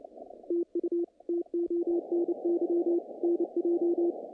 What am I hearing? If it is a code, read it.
Morse code, heard as a keyed tone, about 28 wpm, 340 Hz. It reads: TU TONY NJ
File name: contests / naqp-cw-k5tr-op-k5pi- -2300